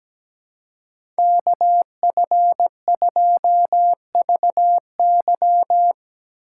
Morse code, heard as a keyed tone, about 17 wpm, 700 Hz. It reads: KF2VY
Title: USA Call Signs - 5 Characters - 2x Character Spacing - 17wpm